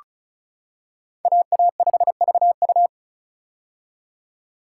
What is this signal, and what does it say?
Morse code, heard as a keyed tone, about 36 wpm, 700 Hz. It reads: AA5VU
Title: K1USN SST QSO Practice - 35wpm